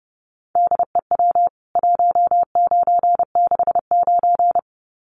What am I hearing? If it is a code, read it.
Morse code, heard as a keyed tone, about 30 wpm, 700 Hz. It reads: DEW 1969
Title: CWT Exchange 30wpm